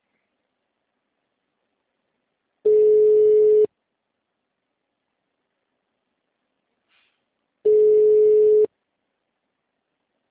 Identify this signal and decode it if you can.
Ringback tone (Europe, 425 Hz).
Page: Guarapuava - PR